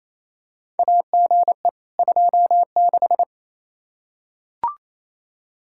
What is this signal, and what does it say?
Morse code, heard as a keyed tone, about 27 wpm, 700 Hz. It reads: AGE 26
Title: Practice QSO - Exchange 3 - More Info - 28wpm